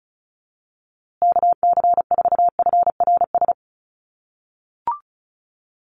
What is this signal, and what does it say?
Morse code, heard as a keyed tone, about 34 wpm, 700 Hz. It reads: KC4FRS
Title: USA Call Signs Repeated 3x 35wpm